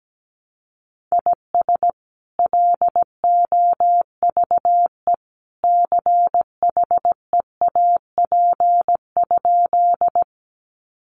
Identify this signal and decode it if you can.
Morse code, heard as a keyed tone, about 17 wpm, 700 Hz. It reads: IS LOVE CHEAP?
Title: Sentences from Top 1200 Words 17wpm